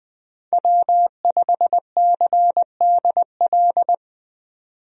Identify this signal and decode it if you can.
Morse code, heard as a keyed tone, about 21 wpm, 700 Hz. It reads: W5CDL
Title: USA Call Signs - 5 Characters - 20wpm